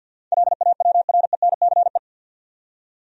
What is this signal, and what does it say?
Morse code, heard as a keyed tone, about 47 wpm, 700 Hz. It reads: LAWRENCE
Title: USA Common Names - 50wpm